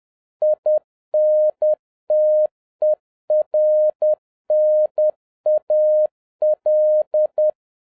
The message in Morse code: INTERNAL

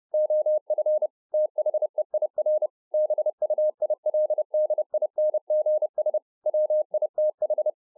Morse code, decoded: OF THEIR BUILDINGS WITH

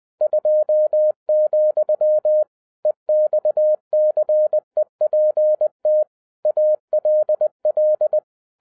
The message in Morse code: 2, EXCEPT ALL

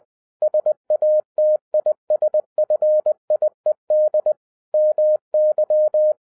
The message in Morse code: SATISFIED MY

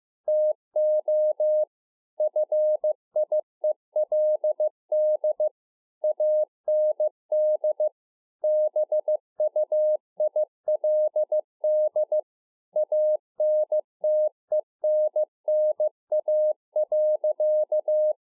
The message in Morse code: TO FIELD AND BUILD ANTENNA.

